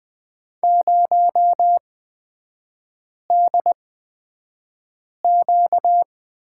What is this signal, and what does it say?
Morse code, heard as a keyed tone, about 20 wpm, spaced as for 5 wpm, 700 Hz. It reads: 0DQ
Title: 255 Lesson 41e - Period - Three Characters 20wpm